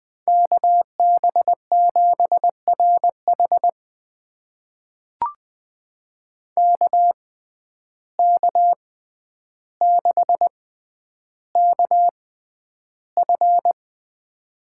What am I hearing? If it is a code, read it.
Morse code, heard as a keyed tone, about 20 wpm, spaced as for 10 wpm, 700 Hz. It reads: KB7RH KK6KF
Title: USA Call Signs - 6x Character Spacing - 20wpm